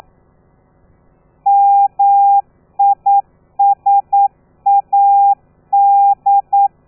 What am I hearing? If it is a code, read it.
Morse code, heard as a keyed tone, about 9 wpm, 800 Hz. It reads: MISAD
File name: telegrafia / i2tfj / Lez13 40